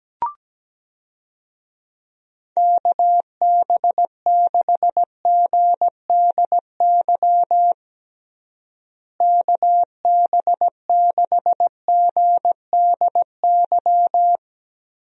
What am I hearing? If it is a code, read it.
Morse code, heard as a keyed tone, about 17 wpm, 700 Hz. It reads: KB6GDY KB6GDY
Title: USA Call Signs - 6 Characters - Repeated 3x - 17wpm